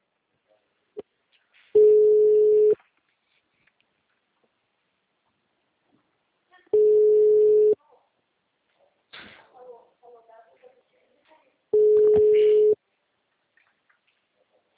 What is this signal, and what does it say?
Ringback tone (Europe, 425 Hz), with speech.